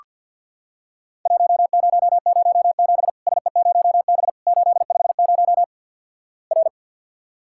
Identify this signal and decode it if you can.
Morse code, heard as a keyed tone, about 47 wpm, 700 Hz. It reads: 1007 SE0B 850 R
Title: Worked All Europe DX Contest - QTC Exchange - 50wpm